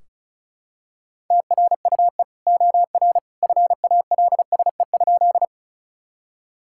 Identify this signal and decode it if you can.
Morse code, heard as a keyed tone, about 34 wpm, 700 Hz. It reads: TRUE OR FALSE?
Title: Sentences from Top 1600 Words 35wpm